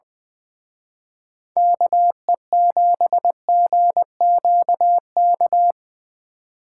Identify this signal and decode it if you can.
Morse code, heard as a keyed tone, about 23 wpm, 700 Hz. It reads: KE7GQK
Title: Speed Racing - USA Call Signs 20wpm